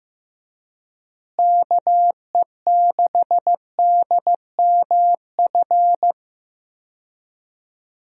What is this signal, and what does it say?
Morse code, heard as a keyed tone, about 16 wpm, 700 Hz. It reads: KE6DMF
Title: USA Call Signs - 6 Characters - Repeated 3x - 15wpm